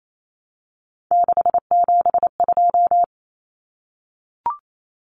Morse code, heard as a keyed tone, about 28 wpm, 700 Hz. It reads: 672
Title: Numbers - 3-Digit 28wpm